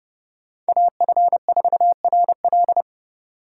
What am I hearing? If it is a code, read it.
Morse code, heard as a keyed tone, about 28 wpm, 700 Hz. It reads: AF4RL